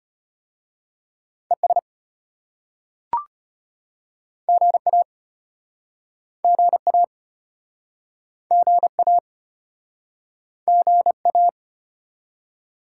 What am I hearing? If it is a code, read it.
Morse code, heard as a keyed tone, about 32 wpm, 700 Hz. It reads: ES GA GA GA GA